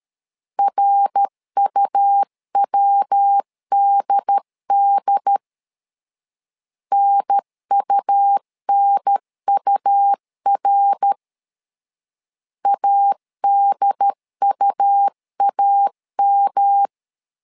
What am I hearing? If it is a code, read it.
Morse code, heard as a keyed tone, about 13 wpm, 800 Hz. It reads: RUWDD NUNUR ADUAM
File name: LEZIONI / 3 MWRO / 3 MWR 12wpm 1711